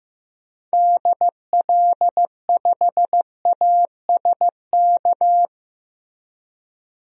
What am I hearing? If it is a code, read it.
Morse code, heard as a keyed tone, about 15 wpm, 700 Hz. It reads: DL5ASK
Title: DXCC-100 Call Signs - 6 Characters - Repeated 3x - 15wpm